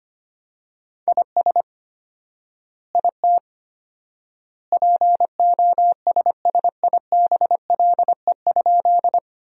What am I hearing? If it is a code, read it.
Morse code, heard as a keyed tone, about 25 wpm, 700 Hz. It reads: IS IT POSSIBLE?
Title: 4x Word-Spacing - Sentences from Top 500 Words - 25wpm